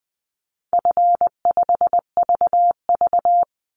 Morse code, heard as a keyed tone, about 20 wpm, 700 Hz. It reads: F5VV
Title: Easy Call Signs - 20wpm